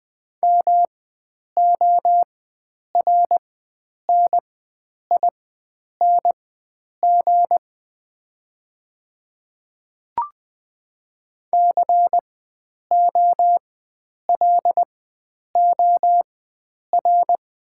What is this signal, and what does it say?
Morse code, heard as a keyed tone, about 20 wpm, spaced as for 9 wpm, 700 Hz. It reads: MORNING COLOR
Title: 4x Character Spacing - Top 500 Words - 20wpm